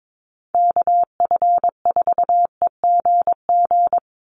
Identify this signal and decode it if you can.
Morse code, heard as a keyed tone, about 22 wpm, 700 Hz. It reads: KF4EGG
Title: USA Call Signs 22wpm